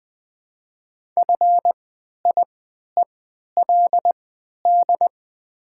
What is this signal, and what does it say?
Morse code, heard as a keyed tone, about 20 wpm, spaced as for 11 wpm, 700 Hz. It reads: FIELD
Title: 3x Character Spacing - Top 500 Words - 20wpm